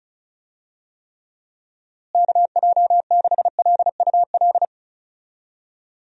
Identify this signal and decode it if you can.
Morse code, heard as a keyed tone, about 34 wpm, 700 Hz. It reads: KJ6LUL
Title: USA Call Signs - 6 Characters - Repeated 2x - 35wpm